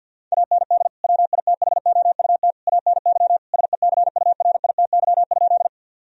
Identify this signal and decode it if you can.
Morse code, heard as a keyed tone, about 50 wpm, 700 Hz. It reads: AND WITHOUT ANY SECURITY?
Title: Sentences from Top 1000 Words - Review Part 4 - 50wpm